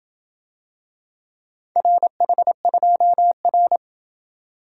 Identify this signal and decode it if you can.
Morse code, heard as a keyed tone, about 27 wpm, 700 Hz. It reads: RH2R